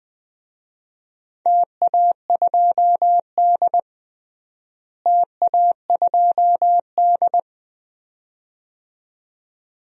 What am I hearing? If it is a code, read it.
Morse code, heard as a keyed tone, about 20 wpm, 700 Hz. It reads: TA2D TA2D
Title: DXCC-100 Call Signs - 4 Characters - Repeated 2x - 20wpm